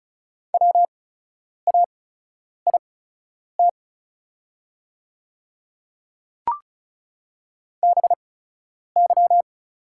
Morse code, heard as a keyed tone, about 36 wpm, spaced as for 10 wpm, 690 Hz. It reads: WAIT BY
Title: 8x Character Spacing - Top 500 Words - 35wpm